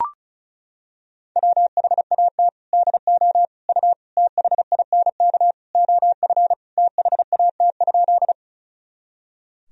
Morse code, heard as a keyed tone, about 34 wpm, 700 Hz. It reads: WHAT DO U THINK OF THAT?
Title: Sentences from Top 200 Words and Abbreviations 35wpm